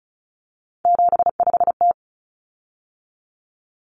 Morse code, heard as a keyed tone, about 34 wpm, 700 Hz. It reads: 75T
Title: Common Cut Numbers - 3-Digit 35wpm